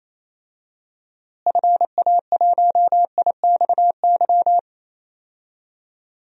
Morse code, heard as a keyed tone, about 28 wpm, 700 Hz. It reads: FA1IXY
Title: DXCC-100 Call Signs - 6 Characters - Repeated 2x - 28wpm